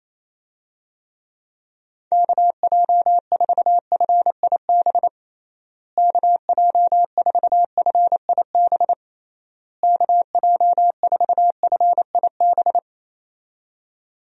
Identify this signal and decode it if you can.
Morse code, heard as a keyed tone, about 28 wpm, 700 Hz. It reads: KJ4FIB KJ4FIB KJ4FIB
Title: USA Call Signs - 6 Characters - Repeated 3x - 28wpm